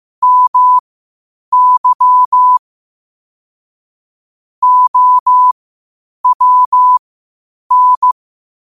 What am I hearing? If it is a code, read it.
Morse code, heard as a keyed tone, about 15 wpm, spaced as for 8 wpm, 1000 Hz. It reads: MY OWN
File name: qotdfiles / 8wpm